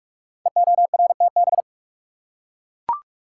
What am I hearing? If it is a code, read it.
Morse code, heard as a keyed tone, about 44 wpm, 700 Hz. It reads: EORTB
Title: Code Groups - 5 Letter:Numbers - 45wpm